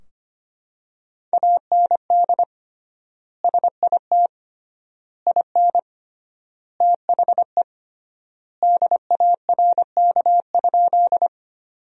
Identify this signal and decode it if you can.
Morse code, heard as a keyed tone, about 25 wpm, 700 Hz. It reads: AND SIT IN THE DARK?
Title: 3x Word-Spacing - Sentences from Top 500 Words - 25wpm